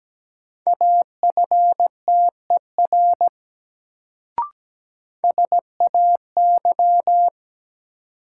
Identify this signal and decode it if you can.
Morse code, heard as a keyed tone, about 17 wpm, 700 Hz. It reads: AFTER SAY
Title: Sentences from Top 200 Words - New Word Drill - 17wpm